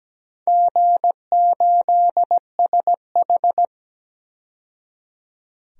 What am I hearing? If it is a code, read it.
Morse code, heard as a keyed tone, about 17 wpm, 700 Hz. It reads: G8SH
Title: DXCC-100 Call Signs - 4 Characters - 17wpm